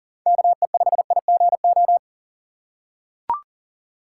Morse code, heard as a keyed tone, about 37 wpm, 700 Hz. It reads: KE5IGO
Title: USA Call Signs - 6 Characters - 4x Character Spacing - 40wpm